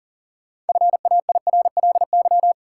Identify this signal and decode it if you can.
Morse code, heard as a keyed tone, about 38 wpm, 700 Hz. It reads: FAIRLY
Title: Sentences from Top 1100 Words - New Word Drill - 40wpm